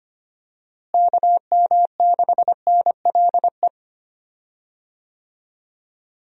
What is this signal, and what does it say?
Morse code, heard as a keyed tone, about 25 wpm, 700 Hz. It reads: KM6NLE
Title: USA Call Signs - 6 Characters - Repeated 3x - 25wpm